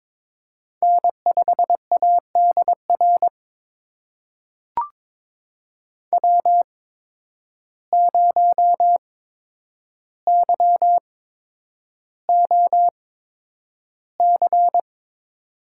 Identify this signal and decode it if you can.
Morse code, heard as a keyed tone, about 23 wpm, spaced as for 10 wpm, 700 Hz. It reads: N5ADR W0YOC
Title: USA Call Signs - 5 Characters - 8x Character Spacing - 22wpm